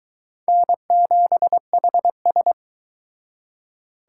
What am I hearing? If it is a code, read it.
Morse code, heard as a keyed tone, about 23 wpm, 700 Hz. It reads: N7HS